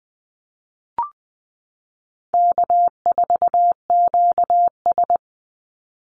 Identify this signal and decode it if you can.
Morse code, heard as a keyed tone, about 20 wpm, 700 Hz. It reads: K4QS
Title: NA SOTA Activator Callsigns 20wpm